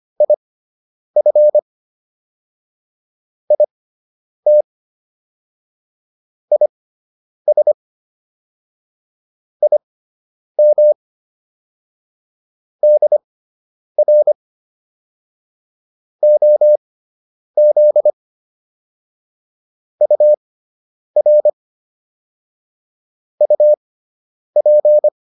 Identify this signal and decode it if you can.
Morse code, heard as a keyed tone, about 25 wpm, spaced as for 9 wpm, 600 Hz. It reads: IF IT IS IM DR OZ UR UP